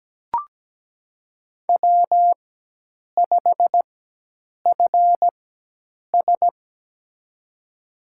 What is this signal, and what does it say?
Morse code, heard as a keyed tone, about 17 wpm, spaced as for 8 wpm, 700 Hz. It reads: W5FS
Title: USA Call Signs - 4x Character Spacing - 17wpm